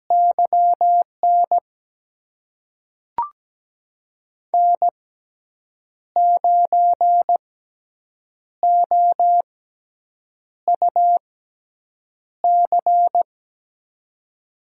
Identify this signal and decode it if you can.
Morse code, heard as a keyed tone, about 17 wpm, spaced as for 7 wpm, 700 Hz. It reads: YN N9OUC
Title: USA Call Signs - 5 Characters - 6x Character Spacing - 17wpm